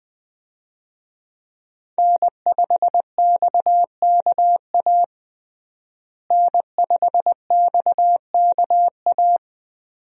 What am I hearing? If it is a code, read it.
Morse code, heard as a keyed tone, about 20 wpm, 700 Hz. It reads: N5XKA N5XKA
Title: USA Call Signs - 5 Characters - Repeated 2x - 20wpm